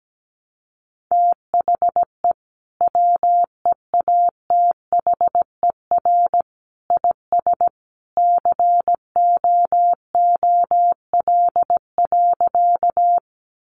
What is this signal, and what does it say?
Morse code, heard as a keyed tone, about 17 wpm, 700 Hz. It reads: THE WEATHER IS COOL.
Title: Sentences from Top 1200 Words 17wpm